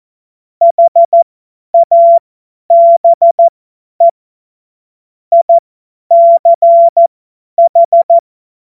Morse code, heard as a keyed tone, about 14 wpm, spaced as for 10 wpm, 680 Hz. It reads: HABE ICH
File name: Morsekurs / Witze / Witz 10. Januar 2023 50 F70